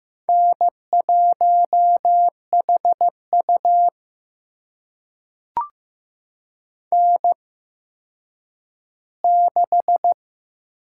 Morse code, heard as a keyed tone, about 15 wpm, spaced as for 9 wpm, 700 Hz. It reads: N1HU N6
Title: USA Call Signs - 4 Characters - 8x Character Spacing - 15wpm